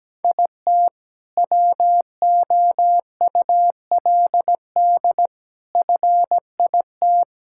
Morse code, heard as a keyed tone, about 17 wpm, 700 Hz. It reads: IT WOULD FIT
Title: Sentences from Top 1000 Words - Review Part 5 - 17wpm